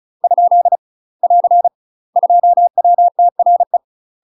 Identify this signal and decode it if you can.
Morse code, heard as a keyed tone, about 34 wpm, 700 Hz. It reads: ? <AR> 2WTRE
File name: UK MASTER QSOs TRAINING / 35WPM / UK MASTER QSOs 0034